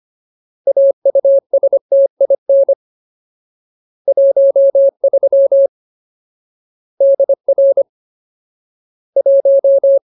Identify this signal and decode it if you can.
Morse code, heard as a keyed tone, about 25 wpm, 550 Hz. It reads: AUSTIN 13 DR 1
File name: Ultimate Word List / 25WPM Ultimate